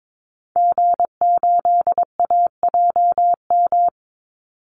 Morse code, heard as a keyed tone, about 22 wpm, 700 Hz. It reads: G8AJM
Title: Call Signs Repeated 2x - 22wpm